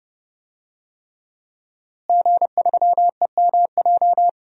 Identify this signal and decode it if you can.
Morse code, heard as a keyed tone, about 30 wpm, 700 Hz. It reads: G3EMJ
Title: DXCC-100 Call Signs - 5 Characters - Repeated 3x - 30wpm